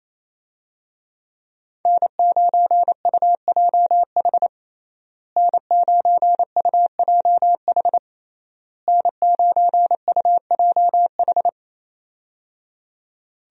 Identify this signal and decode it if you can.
Morse code, heard as a keyed tone, about 28 wpm, 700 Hz. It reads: N9UJH N9UJH N9UJH
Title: USA Call Signs - 5 Characters - Repeated 3x - 28wpm